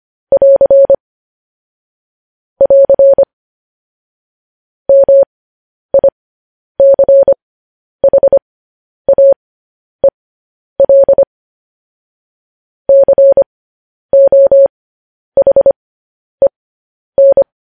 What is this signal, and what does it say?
Morse code, heard as a keyed tone, about 25 wpm, spaced as for 10 wpm, 550 Hz. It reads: <AR> <AR> MICHAEL COHEN